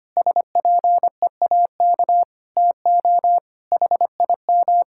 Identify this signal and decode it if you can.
Morse code, heard as a keyed tone, about 25 wpm, 700 Hz. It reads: SPEAK TO HIM
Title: Sentences from Top 1600 Words 25wpm